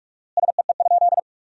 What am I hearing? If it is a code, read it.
Morse code, heard as a keyed tone, about 46 wpm, 700 Hz. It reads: SEE?
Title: 4x Word-Spacing - Sentences from Top 100 Words - 45wpm